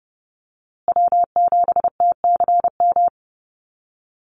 Morse code, heard as a keyed tone, about 30 wpm, 700 Hz. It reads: W7TCM
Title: USA Call Signs Repeated 2x 30wpm